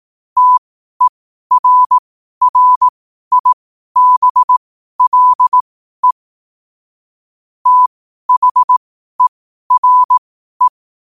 Morse code, heard as a keyed tone, about 18 wpm, spaced as for 12 wpm, 1000 Hz. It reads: TERRIBLE THERE